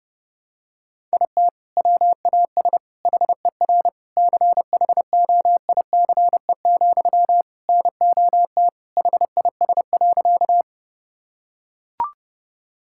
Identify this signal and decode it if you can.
Morse code, heard as a keyed tone, about 32 wpm, 700 Hz. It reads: IT WAS HER CHOICE, NOT HIS.
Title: Sentences from Top 1000 Words - Review Part 6 - 30wpm